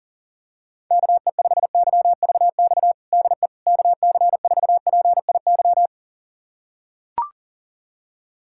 Morse code, heard as a keyed tone, about 40 wpm, 690 Hz. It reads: KE5YVX DE KC4PIY
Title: USA Callsign DE Callsign - 6 Characters - 40wpm